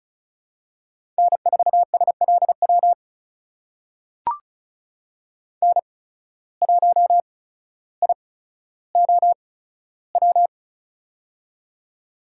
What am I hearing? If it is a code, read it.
Morse code, heard as a keyed tone, about 34 wpm, spaced as for 15 wpm, 700 Hz. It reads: N4SLW N1IOW
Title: USA Call Signs - 5 Characters - 8x Character Spacing - 35wpm